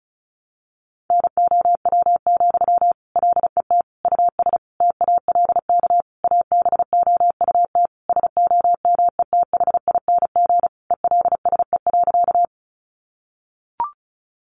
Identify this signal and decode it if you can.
Morse code, heard as a keyed tone, about 34 wpm, 700 Hz. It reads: NOW, LET US TALK ABOUT SOMETHING ELSE.
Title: Sentences from Top 800 Words 35wpm